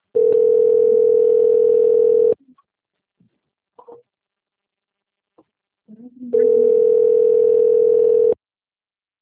Ringback tone (North America), 440 + 480 Hz.
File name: out- - - - - .8514